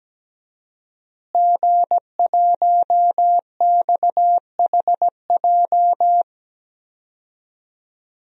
Morse code, heard as a keyed tone, about 17 wpm, 700 Hz. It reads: G1XHJ
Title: DXCC-100 Call Signs - 5 Characters - Repeated 3x - 17wpm